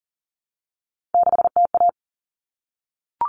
Morse code, heard as a keyed tone, about 40 wpm, 700 Hz. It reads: 6TA
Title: All Cut Numbers - 3-Digit 40wpm